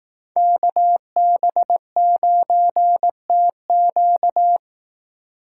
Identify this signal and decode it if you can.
Morse code, heard as a keyed tone, about 18 wpm, 700 Hz. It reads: KB9TQ
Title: Speed Racing - USA 2x2 Call Signs 15wpm